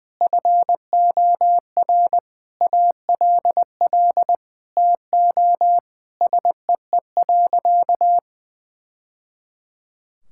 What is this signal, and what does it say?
Morse code, heard as a keyed tone, about 20 wpm, 700 Hz. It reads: FOR ALL TO SEE.
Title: Sentences from Top 1000 Words - Review Part 5 - 20wpm